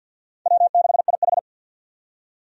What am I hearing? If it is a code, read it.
Morse code, heard as a keyed tone, about 53 wpm, 700 Hz. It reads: W6IH
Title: USA Call Signs - 4 Characters - 50wpm